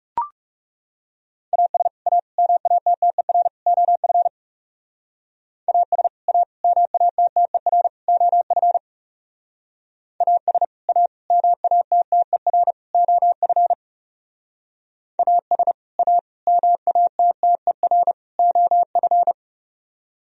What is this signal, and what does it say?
Morse code, heard as a keyed tone, about 38 wpm, 700 Hz. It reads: AS A MATTER OF AS A MATTER OF AS A MATTER OF AS A MATTER OF